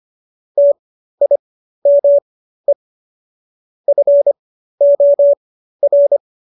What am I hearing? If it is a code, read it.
Morse code, heard as a keyed tone, about 25 wpm, spaced as for 13 wpm, 570 Hz. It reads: TIME FOR